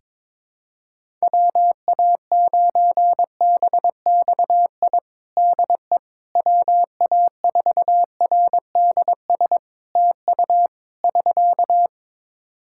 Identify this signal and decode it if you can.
Morse code, heard as a keyed tone, about 22 wpm, 700 Hz. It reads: WA9BXI DE WA4RDS TU <SK>
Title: Practice QSO - Exchange 4 - Ending - 22wpm